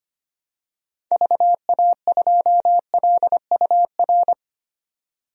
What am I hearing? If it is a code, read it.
Morse code, heard as a keyed tone, about 25 wpm, 700 Hz. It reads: VA2LUR
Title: DXCC-100 Call Signs - 6 Characters - 25wpm